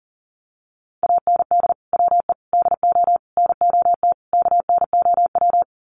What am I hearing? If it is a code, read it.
Morse code, heard as a keyed tone, about 40 wpm, 700 Hz. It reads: AND WE DO NOT KNOW
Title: Sets of 5 Words Encore 40wpm